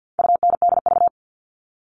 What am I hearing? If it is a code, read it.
Morse code, heard as a keyed tone, about 50 wpm, 700 Hz. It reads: UNDV
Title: All Cut Numbers - 4-Digit 50wpm